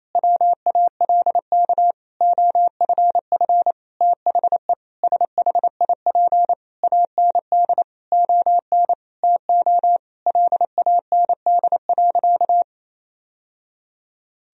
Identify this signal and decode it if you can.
Morse code, heard as a keyed tone, about 28 wpm, 700 Hz. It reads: WALK OFF THE SHIP AND ON TO LAND.
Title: Sentences from Top 1000 Words - Review Part 2 - 28wpm